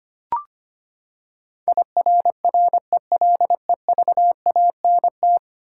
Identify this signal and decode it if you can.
Morse code, heard as a keyed tone, about 25 wpm, 700 Hz. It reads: IRRELEVANT
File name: Ten Letter Words - 25wpm